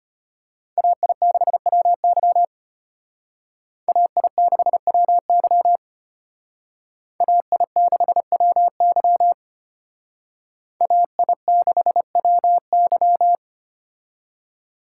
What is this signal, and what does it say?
Morse code, heard as a keyed tone, about 29 wpm, 700 Hz. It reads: AI6WY AI6WY AI6WY AI6WY
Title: Speed Racing - USA 2x2 Call Signs 25wpm